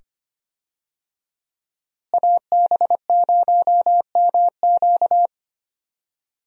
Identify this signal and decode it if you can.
Morse code, heard as a keyed tone, about 25 wpm, 700 Hz. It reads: AB0MQ
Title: USA Call Signs - 5 Characters - Repeated 2x - 25wpm